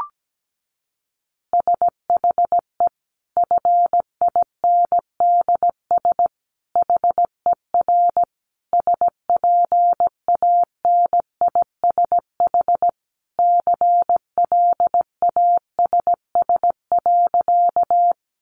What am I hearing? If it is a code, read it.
Morse code, heard as a keyed tone, about 17 wpm, 700 Hz. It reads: SHE FINDS HER SPANISH CLASS.
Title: Additional Easy Phrases 17wpm